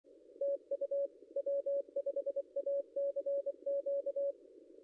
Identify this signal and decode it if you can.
Morse code, heard as a keyed tone, about 24 wpm, 560 Hz. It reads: TU W5ACQ
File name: contests / ss-cw-k5tr-op-n5rz -0800